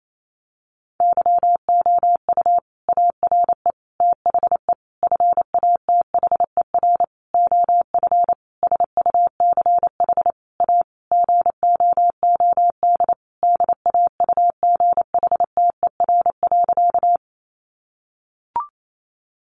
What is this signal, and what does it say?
Morse code, heard as a keyed tone, about 28 wpm, 700 Hz. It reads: YOU ARE THE FATHER OF SUCH A GOOD DAUGHTER.
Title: Sentences from Top 1500 Words 28wpm